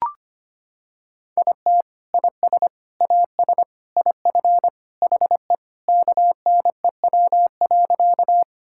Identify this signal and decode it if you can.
Morse code, heard as a keyed tone, about 25 wpm, 700 Hz. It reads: IT IS AS IF HE KNEW.
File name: practice / 25wpm / Sentences with 3-Word Phrases - 25wpm